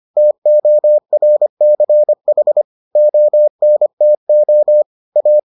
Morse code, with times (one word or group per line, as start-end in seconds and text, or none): 0.2-2.6 TORCH
3.0-4.8 ONTO
5.2-5.4 A